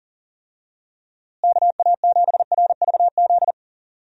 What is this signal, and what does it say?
Morse code, heard as a keyed tone, about 40 wpm, 700 Hz. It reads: KA7RVZ
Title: USA Call Signs - 6 Characters - Repeated 2x - 40wpm